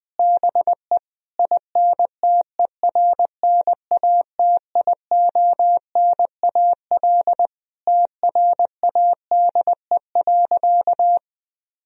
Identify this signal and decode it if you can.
Morse code, heard as a keyed tone, about 20 wpm, 700 Hz. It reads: BE INTERNATIONAL TRADE.
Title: Sentences from Top 1000 Words - Review Part 4 - 20wpm